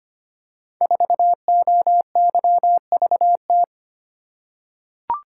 Morse code, heard as a keyed tone, about 25 wpm, 700 Hz. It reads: 4OYVT
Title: Code Groups - 5 Letter:Numbers - 25wpm